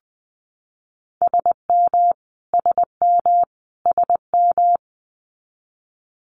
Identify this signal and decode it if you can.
Morse code, heard as a keyed tone, about 20 wpm, 700 Hz. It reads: SM SM SM
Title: Multi-letter Phonemes Repeated 3x 20wpm